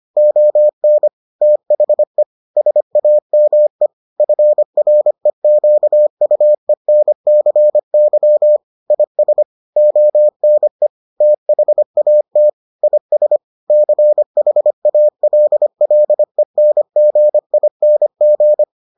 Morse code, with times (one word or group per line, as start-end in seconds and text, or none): 0.2-1.1 ON
1.4-2.2 THE
2.6-3.9 SAME
4.2-8.6 FREQUENCY
8.9-9.4 IS
9.8-10.9 ONE
11.2-12.5 THAT
12.8-13.4 IS
13.7-18.6 CHALLENGING